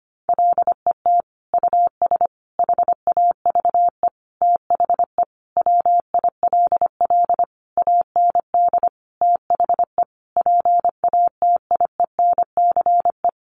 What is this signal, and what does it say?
Morse code, heard as a keyed tone, about 25 wpm, 700 Hz. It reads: LET US HAVE THE WILL AND THE PATIENCE
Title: Sentences from Top 1300 Words 25wpm